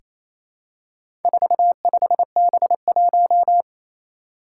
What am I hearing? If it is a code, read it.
Morse code, heard as a keyed tone, about 28 wpm, 700 Hz. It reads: 45B1
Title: Four Challenging Letter-Number Combinations 28wpm